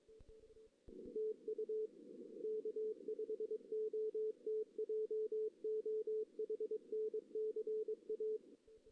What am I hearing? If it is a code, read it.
Morse code, heard as a keyed tone, about 23 wpm, 410 Hz. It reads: TU K5OT JOHN CA
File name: 2016 JAN CW NAQP / 2016 Jan naqp k5ot - 0-2